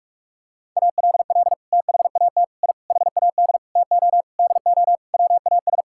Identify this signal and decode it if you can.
Morse code, heard as a keyed tone, about 41 wpm, 700 Hz. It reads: ALL THAT I HAD TO DO WAS
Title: Sentences from Top 1000 Words - Review Part 6 - 45wpm